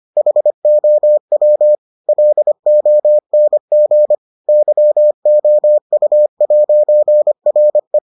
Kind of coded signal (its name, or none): Morse code